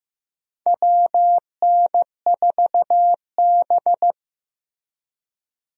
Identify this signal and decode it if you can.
Morse code, heard as a keyed tone, about 15 wpm, 700 Hz. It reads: WN4B
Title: USA Call Signs - 4 Characters - Repeated 3x - 15wpm